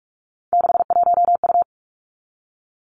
Morse code, heard as a keyed tone, about 46 wpm, 700 Hz. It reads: 61U
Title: Three Challenging Letter-Number Combinations 45wpm